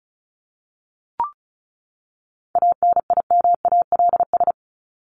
Morse code, heard as a keyed tone, about 34 wpm, 700 Hz. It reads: ANIMALS
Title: ES:S Suffix - Top 100 Words with Base Words 35wpm